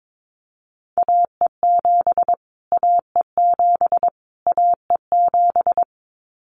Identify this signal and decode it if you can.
Morse code, heard as a keyed tone, about 22 wpm, 700 Hz. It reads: AE7 AE7 AE7
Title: Callsign Prefixes for Top 25 DXCC - Repeated 3x - 22wpm